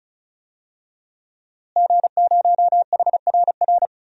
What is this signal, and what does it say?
Morse code, heard as a keyed tone, about 36 wpm, 700 Hz. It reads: G0HRR
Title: DXCC-100 Call Signs - 5 Characters - Repeated 3x - 35wpm